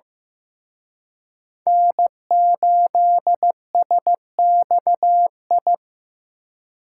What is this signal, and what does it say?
Morse code, heard as a keyed tone, about 18 wpm, 700 Hz. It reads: N8SXI
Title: Speed Racing - USA Call Signs 15wpm